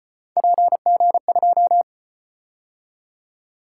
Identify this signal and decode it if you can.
Morse code, heard as a keyed tone, about 35 wpm, 700 Hz. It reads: PG2